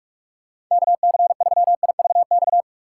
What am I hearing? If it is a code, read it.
Morse code, heard as a keyed tone, about 45 wpm, 700 Hz. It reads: KC3IVX
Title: USA Call Signs - 6 Characters - 3x Character Spacing - 45wpm